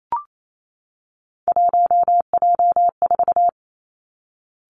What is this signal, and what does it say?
Morse code, heard as a keyed tone, about 28 wpm, 700 Hz. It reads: 1J4
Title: Three Challenging Letter-Number Combinations 28wpm